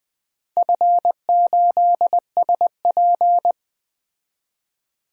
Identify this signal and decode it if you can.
Morse code, heard as a keyed tone, about 20 wpm, 700 Hz. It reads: F8SP
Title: DXCC-100 Call Signs - 4 Characters - 20wpm